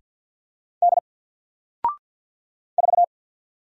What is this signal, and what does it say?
Morse code, heard as a keyed tone, about 49 wpm, 700 Hz. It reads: D 4
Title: Single Character (Letter:Number:Punctuation:Prosign) Alt 50wpm